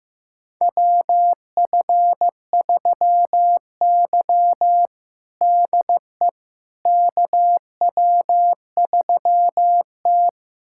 Morse code, heard as a keyed tone, about 16 wpm, 700 Hz. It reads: WF3Y DE KW3T
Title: USA Callsign DE Callsign - 4 Characters - 15wpm